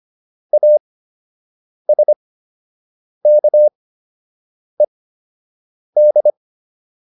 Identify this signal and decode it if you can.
Morse code, heard as a keyed tone, about 25 wpm, spaced as for 7 wpm, 600 Hz. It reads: ASKED